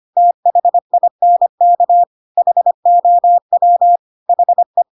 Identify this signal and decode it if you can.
Morse code, heard as a keyed tone, about 25 wpm, 700 Hz. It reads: THINK HOW HE
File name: Pinocchio / Pinocchio ch5 25 25 ik1hge